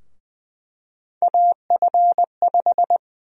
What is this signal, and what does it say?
Morse code, heard as a keyed tone, about 20 wpm, 700 Hz. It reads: AF5
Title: USA Call Sign Prefixes - Repeated 2x - 20wpm